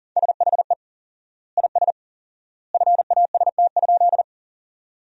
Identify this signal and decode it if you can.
Morse code, heard as a keyed tone, about 40 wpm, 700 Hz. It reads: SHE IS FAST?
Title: 4x Word-Spacing - Sentences from Top 400 Words - 40wpm